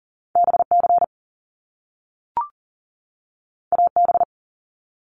Morse code, heard as a keyed tone, about 40 wpm, 700 Hz. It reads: BC AB
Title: Canadian Provinces and Territories 40wpm